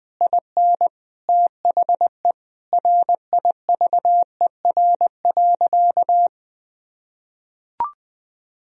Morse code, heard as a keyed tone, about 21 wpm, 700 Hz. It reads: IN THE RIVER.
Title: Sentences from Top 1000 Words - Review Part 4 - 20wpm